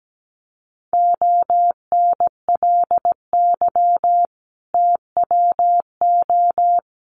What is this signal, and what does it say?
Morse code, heard as a keyed tone, about 17 wpm, 700 Hz. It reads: ONLY TWO
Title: Sentences from Top 200 Words 17wpm